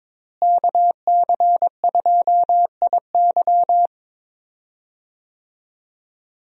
Morse code, heard as a keyed tone, about 22 wpm, 700 Hz. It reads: KC2IY
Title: USA Call Signs - 5 Characters - Repeated 3x - 22wpm